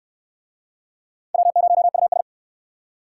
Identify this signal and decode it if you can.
Morse code, heard as a keyed tone, about 67 wpm, 700 Hz. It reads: W0US